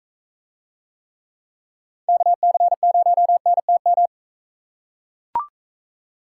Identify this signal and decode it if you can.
Morse code, heard as a keyed tone, about 42 wpm, 700 Hz. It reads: KC0NTM